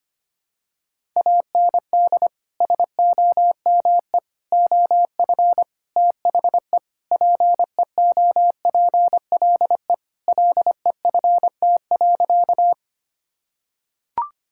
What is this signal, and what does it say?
Morse code, heard as a keyed tone, about 25 wpm, 700 Hz. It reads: AND SOME OF THE PEOPLE LEFT.
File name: practice / 25wpm / Sentences with 4-Word Phrases - 25wpm